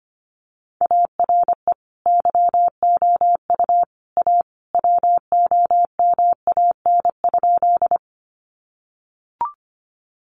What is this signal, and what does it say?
Morse code, heard as a keyed tone, about 25 wpm, 700 Hz. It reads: ARE YOU A WOMAN?
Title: Sentences from Top 1100 Words 25wpm